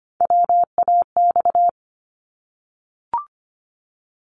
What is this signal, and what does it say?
Morse code, heard as a keyed tone, about 25 wpm, 700 Hz. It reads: WAX
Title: Multi-letter Phonemes Repeated 3x 25wpm